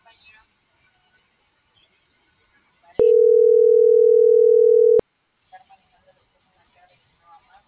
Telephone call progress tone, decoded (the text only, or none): ringback tone